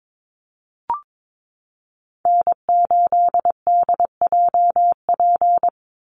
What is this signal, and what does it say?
Morse code, heard as a keyed tone, about 22 wpm, 700 Hz. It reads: N8DJP
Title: USA Call Signs 22wpm